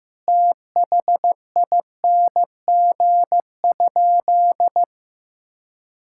Morse code, heard as a keyed tone, about 15 wpm, 700 Hz. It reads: THING?
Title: Sentences from Top 1000 Words - Review Part 5 - 15wpm